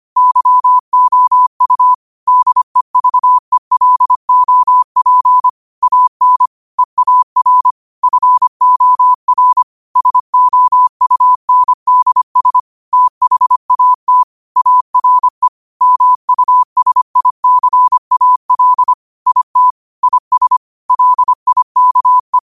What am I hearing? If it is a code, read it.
Morse code, heard as a keyed tone, about 25 wpm, 1000 Hz. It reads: YOU DEVELOP AN EAR FOR SOUNDS THAT ARE MUSICAL IT IS LIKE
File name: qotdfiles / 25wpm